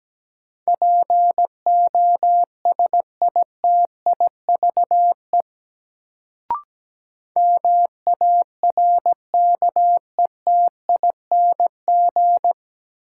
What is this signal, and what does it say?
Morse code, heard as a keyed tone, about 17 wpm, 700 Hz. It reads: POSITIVE MARKETING
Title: Sentences from Top 1000 Words - Review Part 5 - New Word Drill - 17wpm